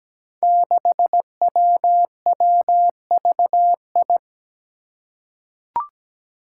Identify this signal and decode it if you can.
Morse code, heard as a keyed tone, about 17 wpm, 700 Hz. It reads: 6WWVI
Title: Five Challenging Letter-Number Combinations 17wpm